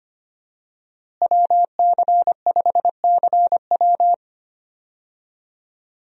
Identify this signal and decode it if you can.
Morse code, heard as a keyed tone, about 25 wpm, 700 Hz. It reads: WC5CW
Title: USA Call Signs - 5 Characters - Repeated 3x - 25wpm